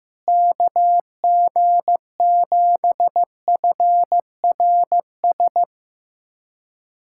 Morse code, heard as a keyed tone, about 15 wpm, 700 Hz. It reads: KG7FRS
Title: USA Callsign DE Callsign 15wpm